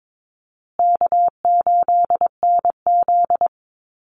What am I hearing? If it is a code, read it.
Morse code, heard as a keyed tone, about 22 wpm, 700 Hz. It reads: K8NZ
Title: Easy Call Signs Repeated 3x - 22wpm